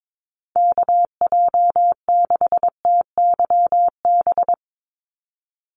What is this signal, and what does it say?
Morse code, heard as a keyed tone, about 22 wpm, 700 Hz. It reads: KJ6TYB
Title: USA Call Signs 22wpm